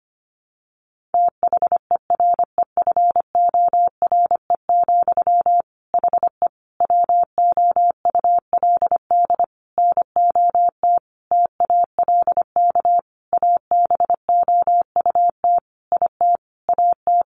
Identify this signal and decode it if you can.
Morse code, heard as a keyed tone, about 25 wpm, 700 Hz. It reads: THEREFORE, HE WOULD NOT TALK ABOUT IT AT
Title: Sentences from Top 800 Words 25wpm